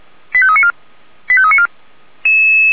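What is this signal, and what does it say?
Five-tone selective call (ZVEI), heard as quick runs of five tones, tones of about 70 ms. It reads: 85394 85394
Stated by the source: ZVEI five-tone selective call 89484, received via SDR on channel 464, logged by feuerwehr-orlamuende fire brigade.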